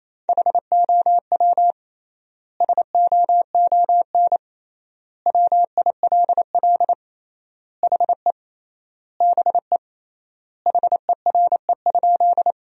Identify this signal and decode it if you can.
Morse code, heard as a keyed tone, about 28 wpm, 700 Hz. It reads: HOW SOON WILL HE BE HERE?
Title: 3x Word-Spacing - Sentences from Top 400 Words - 28wpm